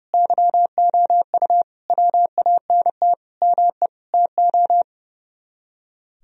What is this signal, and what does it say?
Morse code, heard as a keyed tone, about 38 wpm, 700 Hz. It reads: YOU WANT ME TO